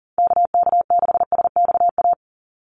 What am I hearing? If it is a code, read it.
Morse code, heard as a keyed tone, about 40 wpm, 700 Hz. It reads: KK6SXA
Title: USA Call Signs 40wpm